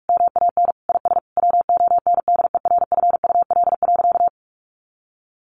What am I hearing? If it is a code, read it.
Morse code, heard as a keyed tone, about 44 wpm, 700 Hz. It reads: MAN IS WONDERFUL.
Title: Sentences from Top 1400 Words 45wpm